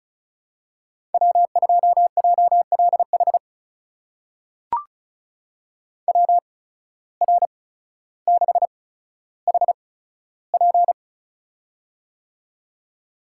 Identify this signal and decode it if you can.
Morse code, heard as a keyed tone, about 36 wpm, spaced as for 15 wpm, 700 Hz. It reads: W2JLH WR6HP
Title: USA Call Signs - 5 Characters - 8x Character Spacing - 35wpm